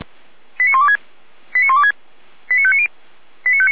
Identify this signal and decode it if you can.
Five-tone selective call (ZVEI), heard as quick runs of five tones, tones of about 80 ms. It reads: 89126 89126 89590